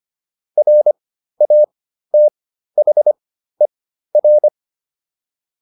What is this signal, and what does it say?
Morse code, heard as a keyed tone, about 25 wpm, spaced as for 13 wpm, 600 Hz. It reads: RATHER